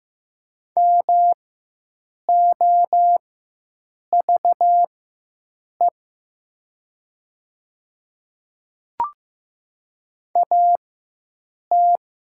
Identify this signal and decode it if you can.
Morse code, heard as a keyed tone, about 15 wpm, spaced as for 7 wpm, 700 Hz. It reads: MOVE AT
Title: 4x Character Spacing - Top 200 Words - 15wpm